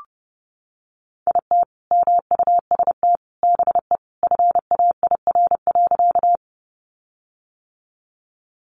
Morse code, heard as a keyed tone, about 30 wpm, 700 Hz. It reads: IT MUST BE FAIR.
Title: Sentences from Top 1300 Words 30wpm